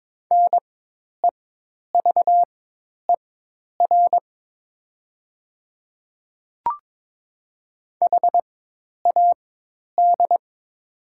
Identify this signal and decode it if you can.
Morse code, heard as a keyed tone, about 22 wpm, spaced as for 10 wpm, 700 Hz. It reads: NEVER HAD
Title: 4x Character Spacing - Top 300 Words - 22wpm